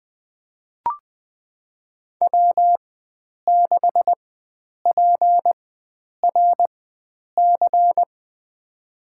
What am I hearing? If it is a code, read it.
Morse code, heard as a keyed tone, about 20 wpm, spaced as for 9 wpm, 700 Hz. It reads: W6PRC
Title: USA Call Signs - 4x Character Spacing - 20wpm